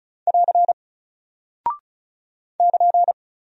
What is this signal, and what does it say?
Morse code, heard as a keyed tone, about 36 wpm, 700 Hz. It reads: <AR> <KN>
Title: Single Character (Letter:Number:Punctuation:Prosign) 35wpm